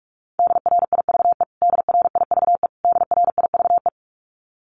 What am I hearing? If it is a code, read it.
Morse code, heard as a keyed tone, about 44 wpm, 700 Hz. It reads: DRIVE DRIVE DRIVE
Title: Top 500 Words Repeated 45wpm